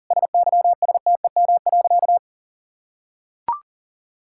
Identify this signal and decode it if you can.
Morse code, heard as a keyed tone, about 40 wpm, 690 Hz. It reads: SYSTEM.
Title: Sentences from Top 1000 Words - Review Part 6 - 40wpm